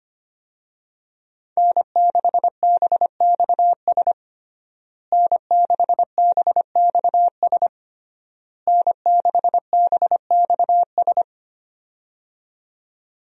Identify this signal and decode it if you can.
Morse code, heard as a keyed tone, about 25 wpm, 700 Hz. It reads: N6BXS N6BXS N6BXS
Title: USA Call Signs - 5 Characters - Repeated 3x - 25wpm